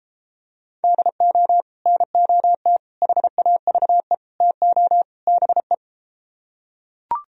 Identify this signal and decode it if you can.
Morse code, heard as a keyed tone, about 32 wpm, 700 Hz. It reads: DO NOT HAVE TO BE